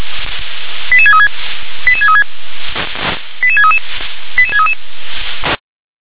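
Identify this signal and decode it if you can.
Five-tone selective call (ZVEI), heard as quick runs of five tones, tones of about 70 ms. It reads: 88636 88636 88633 88633
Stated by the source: ZVEI five-tone call 88631 received via SDR.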